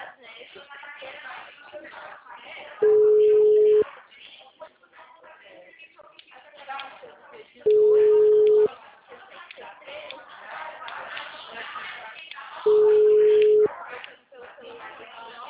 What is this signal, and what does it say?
Ringback tone (Europe, 425 Hz), with speech.
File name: out- -2003- - - .78318